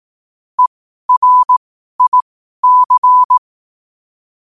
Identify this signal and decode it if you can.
Morse code, heard as a keyed tone, about 18 wpm, spaced as for 12 wpm, 1000 Hz. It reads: ERIC
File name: qotdfiles / 12wpm